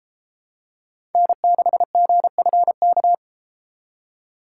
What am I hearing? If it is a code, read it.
Morse code, heard as a keyed tone, about 33 wpm, 700 Hz. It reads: N6GFK